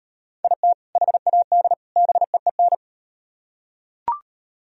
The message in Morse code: IT HAD BEEN

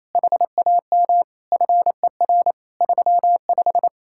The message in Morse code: HAM FER 35

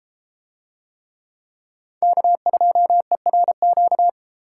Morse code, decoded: K2ERQ

